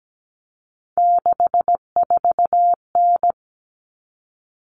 Morse code, heard as a keyed tone, about 17 wpm, 700 Hz. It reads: 64N
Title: All Cut Numbers - 3-Digit 17wpm